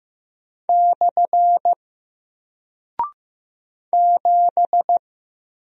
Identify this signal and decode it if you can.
Morse code, heard as a keyed tone, about 15 wpm, 700 Hz. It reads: / 7
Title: Single Character (Letter:Number:Punctuation:Prosign) Alt 15wpm